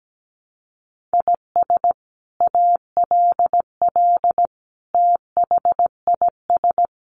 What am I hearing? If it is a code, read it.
Morse code, heard as a keyed tone, about 17 wpm, 700 Hz. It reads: IS ALL THIS
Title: Sentences from Top 1400 Words 17wpm